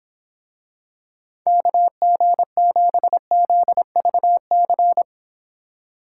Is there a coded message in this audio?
Yes: Morse code from 1.5 to 5.0 s, heard as a keyed tone, about 26 wpm, 700 Hz. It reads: KG7ZVC